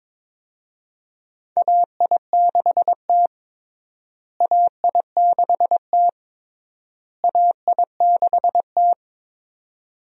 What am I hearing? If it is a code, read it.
Morse code, heard as a keyed tone, about 22 wpm, 700 Hz. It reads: AI6T AI6T AI6T
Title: USA Call Signs - 4 Characters - Repeated 3x - 22wpm